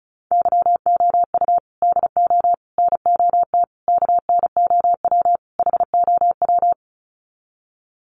Morse code, heard as a keyed tone, about 34 wpm, 700 Hz. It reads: YOU DO NOT KNOW HOW
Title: Sets of 5 Words Encore 35wpm